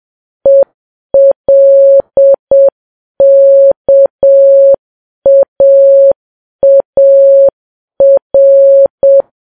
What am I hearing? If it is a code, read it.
Morse code, heard as a keyed tone, about 7 wpm, 550 Hz. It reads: ELKAAR
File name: Files Web tls / Morse on9fv / morse7kps